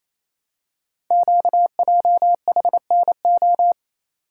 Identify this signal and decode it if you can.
Morse code, heard as a keyed tone, about 28 wpm, 700 Hz. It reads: QJHNO
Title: Code Groups - 5 Letters - 28wpm